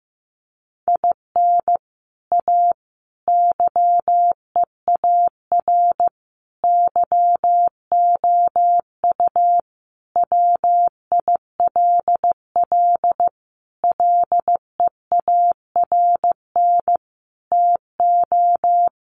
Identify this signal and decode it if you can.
Morse code, heard as a keyed tone, about 15 wpm, 700 Hz. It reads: IN A YEAR YOU WILL LEARN TO
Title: Sentences from Top 600 Words 15wpm